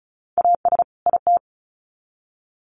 Morse code, heard as a keyed tone, about 34 wpm, 700 Hz. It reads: AS IT
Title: Sets of 2 Words 35wpm